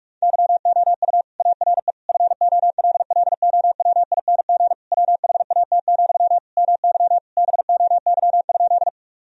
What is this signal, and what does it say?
Morse code, heard as a keyed tone, about 45 wpm, 700 Hz. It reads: YOU ARE FOLLOWING WHAT, MY BOY?
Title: Sentences from Top 1000 Words - Review Part 4 - 45wpm